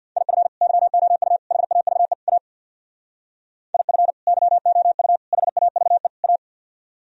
Morse code, heard as a keyed tone, about 64 wpm, 700 Hz. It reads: IF YOU HAVE A IF YOU HAVE A